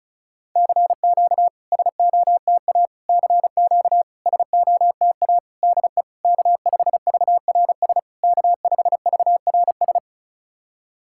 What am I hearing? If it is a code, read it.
Morse code, heard as a keyed tone, about 36 wpm, 700 Hz. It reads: CQ SOTA CQ SOTA DE K5VRS K5VRS
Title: SOTA QSO Practice 35wpm